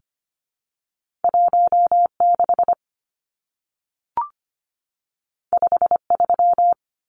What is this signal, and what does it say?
Morse code, heard as a keyed tone, about 25 wpm, 700 Hz. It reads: 1 6 5 3
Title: Numbers - 2-Digit 25wpm